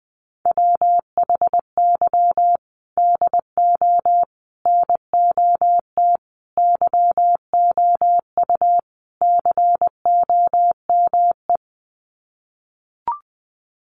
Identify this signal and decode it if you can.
Morse code, heard as a keyed tone, about 20 wpm, 700 Hz. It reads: WHY DO NOT YOU COME
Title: Sets of 5 Words Encore2 20wpm